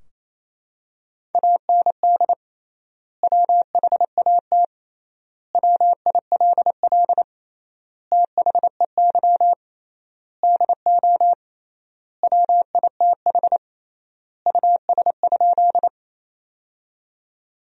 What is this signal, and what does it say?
Morse code, heard as a keyed tone, about 28 wpm, 700 Hz. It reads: AND WHAT WILL THEY DO WITH US?
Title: 3x Word-Spacing - Sentences from Top 200 Words - 28wpm